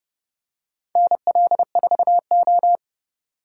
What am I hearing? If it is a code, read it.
Morse code, heard as a keyed tone, about 29 wpm, 700 Hz. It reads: NL4O